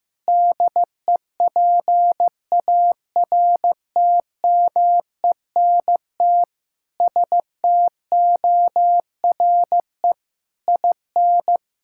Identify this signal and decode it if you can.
Morse code, heard as a keyed tone, about 15 wpm, 700 Hz. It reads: DEPARTMENT STORE IN
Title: Sentences from Top 1000 Words - Review Part 5 - 15wpm